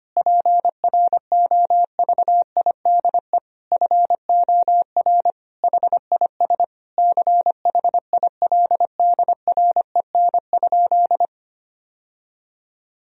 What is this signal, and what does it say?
Morse code, heard as a keyed tone, about 25 wpm, 700 Hz. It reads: PROVIDE FOR HIS CHILDREN?
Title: Sentences from Top 1000 Words - Review Part 3 - 25wpm